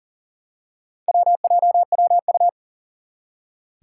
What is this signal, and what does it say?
Morse code, heard as a keyed tone, about 38 wpm, 700 Hz. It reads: WJWU
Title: Four Challenging Letter-Number Combinations 40wpm